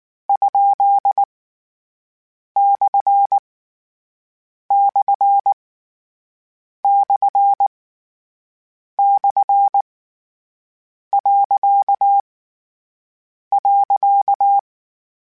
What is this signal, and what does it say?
Morse code, heard as a keyed tone, about 19 wpm, 800 Hz. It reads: ? / / / / . .
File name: Misc / Misc-mp3s / Punct @18